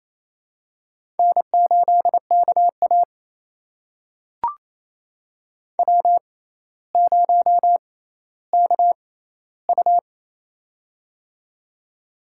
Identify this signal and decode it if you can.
Morse code, heard as a keyed tone, about 28 wpm, spaced as for 14 wpm, 700 Hz. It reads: N8KA W0KU
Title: USA Call Signs - 4 Characters - 6x Character Spacing - 28wpm